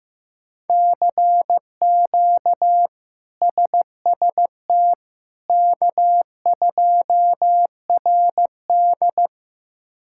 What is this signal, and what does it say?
Morse code, heard as a keyed tone, about 15 wpm, 700 Hz. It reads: CQ SST K2RD
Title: K1USN SST QSO Practice - 15wpm